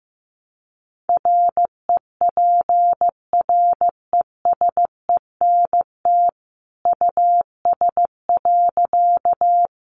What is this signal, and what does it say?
Morse code, heard as a keyed tone, about 15 wpm, 700 Hz. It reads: REPRESENT US.
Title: Sentences from Top 1200 Words 15wpm